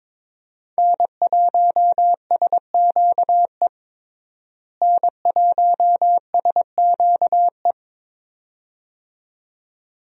Morse code, heard as a keyed tone, about 22 wpm, 700 Hz. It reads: N1SQE N1SQE
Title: USA Call Signs - 5 Characters - Repeated 2x - 22wpm